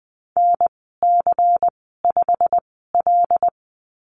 Morse code, heard as a keyed tone, about 20 wpm, spaced as for 15 wpm, 700 Hz. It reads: NC5L
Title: 154 Lesson 25g - Letter p - Call Signs 20wpm